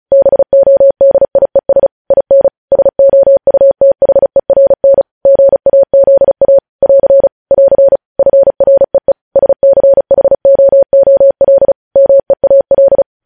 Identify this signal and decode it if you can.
Morse code, heard as a keyed tone, about 36 wpm, 550 Hz. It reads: BODIES IN SOUTHERN GAZA <AR> <AR> FREE SCHOOL MEAL
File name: podcast / News headlines-1x-35x35